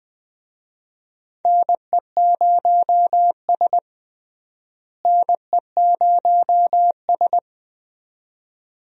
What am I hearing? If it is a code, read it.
Morse code, heard as a keyed tone, about 20 wpm, 700 Hz. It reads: NE0S NE0S
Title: USA Call Signs - 4 Characters - Repeated 2x - 20wpm